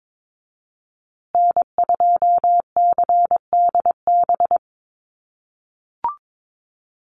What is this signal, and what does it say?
Morse code, heard as a keyed tone, about 22 wpm, 700 Hz. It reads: N2CDB
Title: USA Call Signs Repeated 2x 22wpm